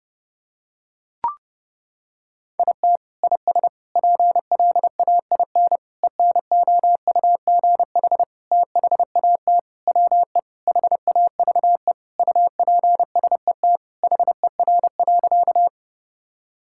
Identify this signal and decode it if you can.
Morse code, heard as a keyed tone, about 30 wpm, 700 Hz. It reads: IT IS PLAIN ENOUGH THAT WE HAVE UPSET HER.
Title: Sentences from Top 1000 Words - Review Part 5 - 30wpm